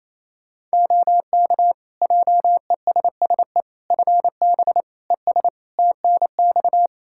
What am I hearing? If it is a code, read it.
Morse code, heard as a keyed tone, about 27 wpm, 700 Hz. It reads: OK JESSE FB ES TNX
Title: Practice QSO - Exchange 2 - Info - 28wpm